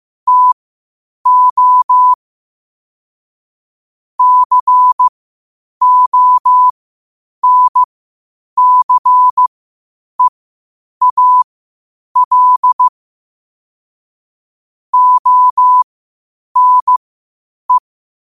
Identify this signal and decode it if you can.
Morse code, heard as a keyed tone, about 15 wpm, spaced as for 8 wpm, 1000 Hz. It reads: TO CONCEAL ONE
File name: qotdfiles / 8wpm